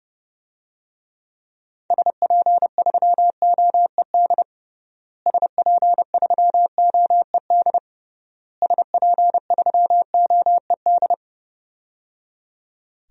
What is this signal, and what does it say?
Morse code, heard as a keyed tone, about 30 wpm, 700 Hz. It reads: SP3OED SP3OED SP3OED
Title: DXCC-100 Call Signs - 6 Characters - Repeated 3x - 30wpm